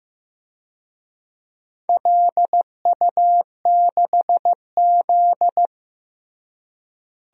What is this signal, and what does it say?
Morse code, heard as a keyed tone, about 15 wpm, 700 Hz. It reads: LU6Z
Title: DXCC-100 Call Signs - 4 Characters - Repeated 2x - 15wpm